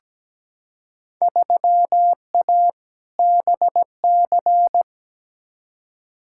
Morse code, heard as a keyed tone, about 17 wpm, 700 Hz. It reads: 3A BC
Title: Field Day Exchanges - 17wpm